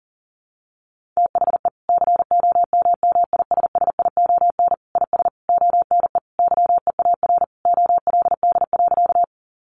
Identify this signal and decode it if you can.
Morse code, heard as a keyed tone, about 40 wpm, 700 Hz. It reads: THE COMMISSION IS ONE YEAR OLD.
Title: Sentences from Top 1400 Words 40wpm